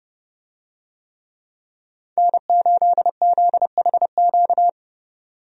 Morse code, heard as a keyed tone, about 30 wpm, 700 Hz. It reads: N8ZHQ
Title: USA Call Signs - 5 Characters - 30wpm